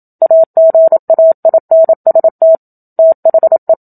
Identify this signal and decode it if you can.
Morse code, heard as a keyed tone, about 27 wpm, 650 Hz. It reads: AGAINST THE
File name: Book Two / 01-Chapter 1 - 27 WPM